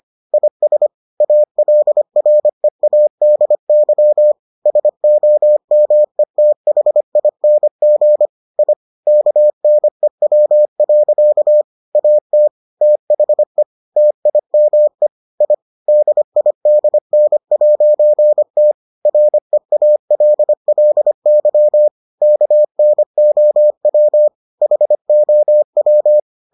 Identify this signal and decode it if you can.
Morse code, heard as a keyed tone, about 25 wpm, 600 Hz. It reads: IS ALREADY SOMETHING I KNEW. AT THE TIME I DIDN'T REALLY KNOW HOW